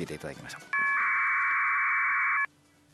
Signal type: packet radio (AX.25, as in APRS)